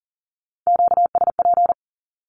Morse code, heard as a keyed tone, about 40 wpm, 700 Hz. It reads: QSP
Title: Q-Codes 40wpm